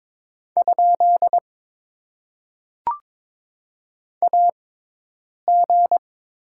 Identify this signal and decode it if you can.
Morse code, heard as a keyed tone, about 21 wpm, spaced as for 8 wpm, 700 Hz. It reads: ? AG
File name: Top 90 QSO Elements - 6x Character Spacing - 22wpm